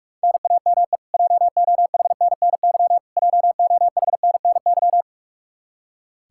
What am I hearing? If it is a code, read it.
Morse code, heard as a keyed tone, about 46 wpm, 700 Hz. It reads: NAME JOHNNY JOHNNY
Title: Practice QSO - Exchange 1 - Beginning - 45wpm